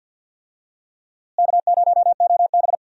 Morse code, heard as a keyed tone, about 52 wpm, 700 Hz. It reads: K0OB
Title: USA Call Signs - 4 Characters - Repeated 3x - 50wpm